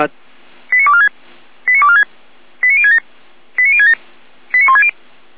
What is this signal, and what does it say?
Five-tone selective call (ZVEI), heard as quick runs of five tones, tones of about 70 ms, with speech. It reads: 89246 89246 89067 89067 89160